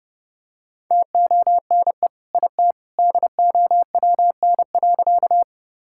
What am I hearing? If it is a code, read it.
Morse code, heard as a keyed tone, about 30 wpm, 700 Hz. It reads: TONE IT DOWN.
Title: Sentences from Top 1600 Words 30wpm